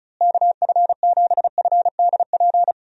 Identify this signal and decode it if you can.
Morse code, heard as a keyed tone, about 34 wpm, 700 Hz. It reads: KF7FDP
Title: USA Call Signs - 6 Characters - Repeated 3x - 35wpm